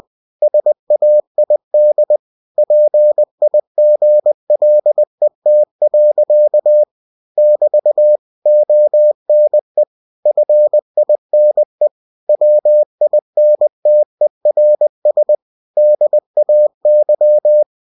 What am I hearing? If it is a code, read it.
Morse code, heard as a keyed tone, about 20 wpm, 600 Hz. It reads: SAID PIGLET. <BT> ONE FINE WINTERS DAY
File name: cw-books / winnie-the-pooh / winnie-20wpm-page-0021